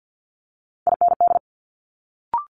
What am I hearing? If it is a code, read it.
Morse code, heard as a keyed tone, about 50 wpm, 700 Hz. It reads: IND
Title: Multi-letter Phonemes Quick Fire 50wpm